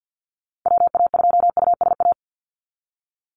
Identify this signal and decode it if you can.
Morse code, heard as a keyed tone, about 50 wpm, 700 Hz. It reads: WA2USA
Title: NA SOTA Activator Callsigns 50wpm